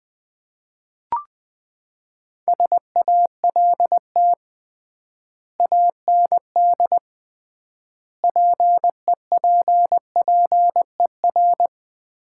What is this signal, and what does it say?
Morse code, heard as a keyed tone, about 19 wpm, 700 Hz. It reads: SALT AND PEPPER
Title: 3x Word-Spacing - Sets of 5 Words - 20wpm